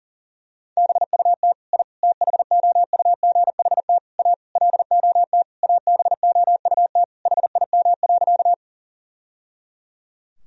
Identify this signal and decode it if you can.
Morse code, heard as a keyed tone, about 40 wpm, 700 Hz. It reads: BUT I THOUGHT A LOT ABOUT HIM.
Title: Sentences from Top 1000 Words - Review Part 3 - 40wpm